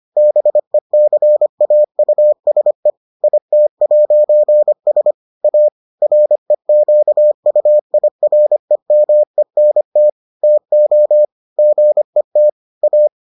Morse code, heard as a keyed tone, about 25 wpm, 600 Hz. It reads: BECAUSE IT'S A REQUIREMENT TO GET A